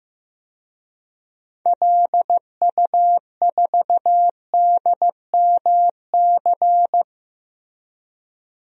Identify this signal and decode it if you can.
Morse code, heard as a keyed tone, about 15 wpm, 700 Hz. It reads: LU4DMC
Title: DXCC-100 Call Signs - 6 Characters - 15wpm